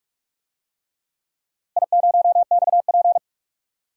Morse code, heard as a keyed tone, about 47 wpm, 700 Hz. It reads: I0XP
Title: DXCC-100 Call Signs - 4 Characters - Repeated 3x - 45wpm